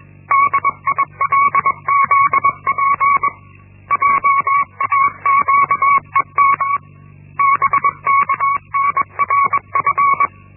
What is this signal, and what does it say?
Morse code, heard as a keyed tone, about 21 wpm, 1120 Hz. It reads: DILZP JAQED BKNRF
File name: Linksmakalnis-1